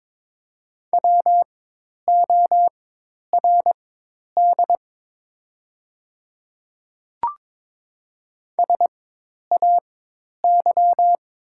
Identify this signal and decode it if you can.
Morse code, heard as a keyed tone, about 22 wpm, spaced as for 10 wpm, 690 Hz. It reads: WORD SAY
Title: 4x Character Spacing - Top 300 Words - 22wpm